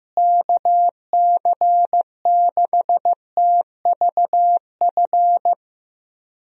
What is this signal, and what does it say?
Morse code, heard as a keyed tone, about 15 wpm, 700 Hz. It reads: KC6TVF
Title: USA Call Signs - 6 Characters - Repeated 2x - 15wpm